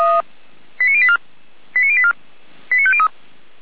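Five-tone selective call (ZVEI), heard as quick runs of five tones, tones of about 70 ms. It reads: 89084 89084 89503
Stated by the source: ZVEI five-tone call 89120 received on an SDR.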